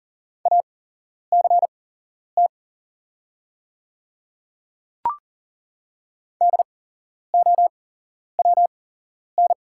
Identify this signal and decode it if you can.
Morse code, heard as a keyed tone, about 41 wpm, spaced as for 11 wpm, 700 Hz. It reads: ACT DOWN
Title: 8x Character Spacing - Top 300 Words - 40wpm